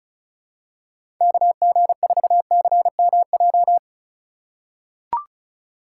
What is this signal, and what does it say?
Morse code, heard as a keyed tone, about 36 wpm, 690 Hz. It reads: KG4CMJ
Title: USA Call Signs - 6 Characters - 4x Character Spacing - 35wpm